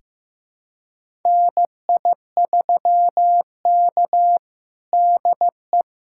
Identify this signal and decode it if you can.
Morse code, heard as a keyed tone, about 15 wpm, 700 Hz. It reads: NI3K DE
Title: USA Callsign DE Callsign - 4 Characters - 15wpm